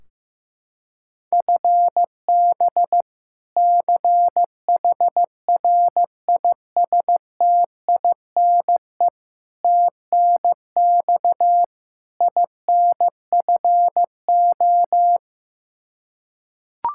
Morse code, heard as a keyed tone, about 15 wpm, 700 Hz. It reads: FB CHRISTINE TNX INFO
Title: Practice QSO - Exchange 3 - More Info - 15wpm